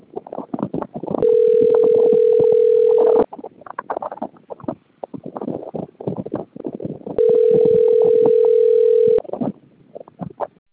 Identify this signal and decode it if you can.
Ringback tone (North America), 440 + 480 Hz.